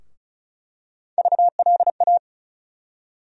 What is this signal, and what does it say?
Morse code, heard as a keyed tone, about 36 wpm, 700 Hz. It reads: VLA
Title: USA Call Sign Postfixes - 3 Letters 35wpm